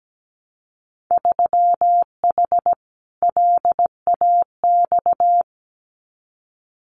Morse code, heard as a keyed tone, about 17 wpm, 700 Hz. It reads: 3H LAX
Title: Winter Field Day Exchanges 17wpm